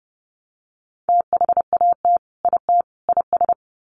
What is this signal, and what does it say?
Morse code, heard as a keyed tone, about 30 wpm, 700 Hz. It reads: THAT IT IS
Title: Sets of 3 Words 30wpm